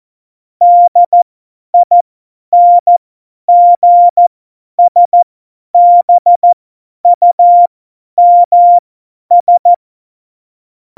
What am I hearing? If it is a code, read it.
Morse code, heard as a keyed tone, about 14 wpm, spaced as for 10 wpm, 700 Hz. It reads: DINGSBUMS
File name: Morsekurs / Witze / Witz 18 Juli 17, Tempo 50 F70